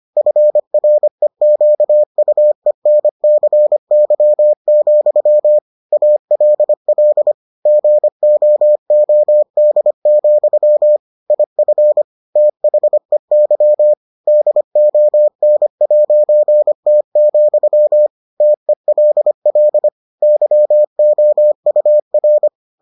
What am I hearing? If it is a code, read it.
Morse code, heard as a keyed tone, about 25 wpm, 600 Hz. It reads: FREQUENCY, ALL GOOD, IF THEY DON'T, TELL YOUR